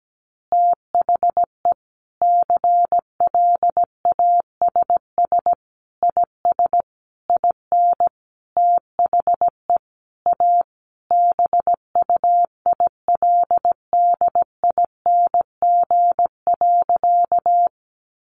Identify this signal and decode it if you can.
Morse code, heard as a keyed tone, about 17 wpm, 700 Hz. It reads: THE CLASS IS IN THE A BUILDING.
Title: Additional Easy Phrases 17wpm